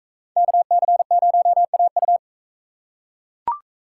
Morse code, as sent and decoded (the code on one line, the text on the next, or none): -.- -.-. ----- .- ..-
KC0AU